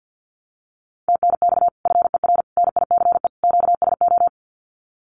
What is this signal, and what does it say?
Morse code, heard as a keyed tone, about 50 wpm, 700 Hz. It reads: TNX FER NICE QSO
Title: Common QSO Phrases Encore 50wpm